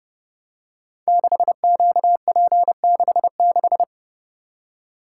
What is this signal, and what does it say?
Morse code, heard as a keyed tone, about 28 wpm, 700 Hz. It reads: 6QP66
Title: Code Groups - 5 Letter:Numbers - 30wpm